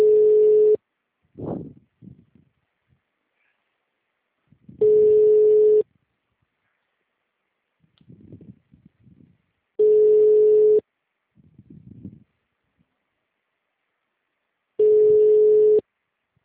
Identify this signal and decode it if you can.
Ringback tone (Europe, 425 Hz).
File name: out- -2006- - - .1179